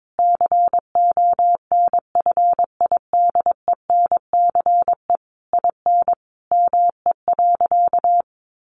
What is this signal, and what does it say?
Morse code, heard as a keyed tone, about 22 wpm, 700 Hz. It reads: CONFIDENCE IN ME.
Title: Sentences from Top 1500 Words 22wpm